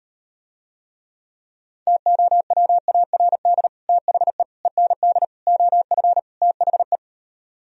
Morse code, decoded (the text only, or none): TOWARD THE END OF THE